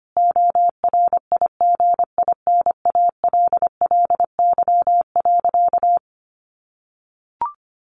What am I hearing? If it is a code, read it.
Morse code, heard as a keyed tone, about 25 wpm, 700 Hz. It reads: ORIGINALLY.
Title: Sentences from Top 1500 Words 25wpm